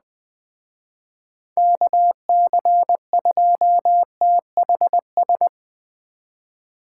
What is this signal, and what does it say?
Morse code, heard as a keyed tone, about 20 wpm, 700 Hz. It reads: KC2THS
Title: Speed Racing - USA Call Signs 17wpm